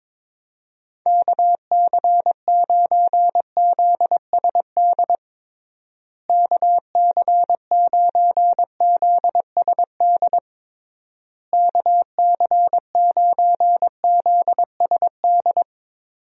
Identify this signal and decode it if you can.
Morse code, heard as a keyed tone, about 21 wpm, 700 Hz. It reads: KC9ZSD KC9ZSD KC9ZSD
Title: USA Call Signs - 6 Characters - Repeated 3x - 22wpm